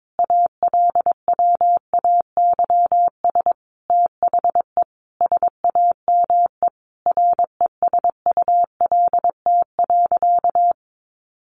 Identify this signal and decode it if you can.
Morse code, heard as a keyed tone, about 22 wpm, 700 Hz. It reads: ALWAYS THE SAME RESULT.
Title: Sentences from Top 700 Words 22wpm